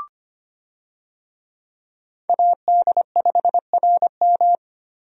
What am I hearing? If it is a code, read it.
Morse code, heard as a keyed tone, about 25 wpm, 700 Hz. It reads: AD5RM
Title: USA Call Signs - 5 Characters - Repeated 2x - 25wpm